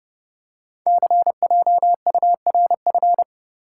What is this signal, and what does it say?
Morse code, heard as a keyed tone, about 30 wpm, 690 Hz. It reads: CJURF
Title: Code Groups - 5 Letters - 30wpm